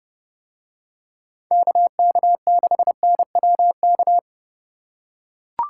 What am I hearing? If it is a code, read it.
Morse code, heard as a keyed tone, about 28 wpm, 700 Hz. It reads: KK6NWK